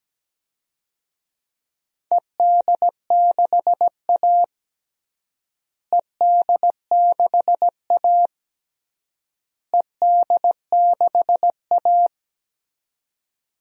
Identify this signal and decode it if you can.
Morse code, heard as a keyed tone, about 17 wpm, 700 Hz. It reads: ED6A ED6A ED6A
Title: DXCC-100 Call Signs - 4 Characters - Repeated 3x - 17wpm